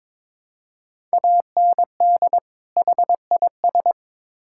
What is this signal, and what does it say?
Morse code, heard as a keyed tone, about 22 wpm, 700 Hz. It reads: AND HIS
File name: Build A Phrase - 2-Word Phrases - 22wpm